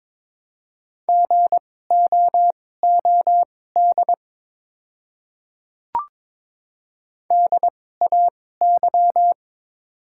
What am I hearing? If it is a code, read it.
Morse code, heard as a keyed tone, about 22 wpm, spaced as for 16 wpm, 700 Hz. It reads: GOOD DAY
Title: 2x Character Spacing - Top 300 Words - 22wpm